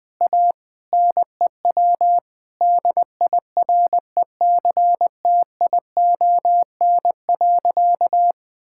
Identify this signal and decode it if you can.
Morse code, heard as a keyed tone, about 20 wpm, 700 Hz. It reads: A NEW DIRECTION.
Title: Sentences from Top 1000 Words - Review Part 5 - 20wpm